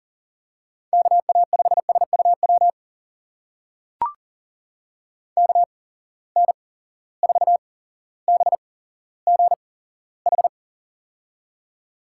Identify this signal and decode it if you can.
Morse code, heard as a keyed tone, about 40 wpm, spaced as for 17 wpm, 700 Hz. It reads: KA5SUW KN4BGH
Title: USA Call Signs - 6 Characters - 8x Character Spacing - 40wpm